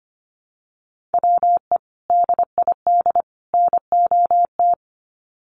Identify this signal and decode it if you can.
Morse code, heard as a keyed tone, about 25 wpm, 700 Hz. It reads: WE DID NOT
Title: Sets of 3 Words 25wpm